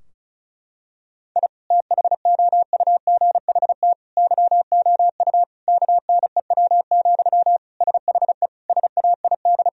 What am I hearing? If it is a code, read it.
Morse code, heard as a keyed tone, about 34 wpm, 700 Hz. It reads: I THOUGHT YOU KNEW, SHE SAID
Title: Sentences from Top 1000 Words - Review Part 4 - 35wpm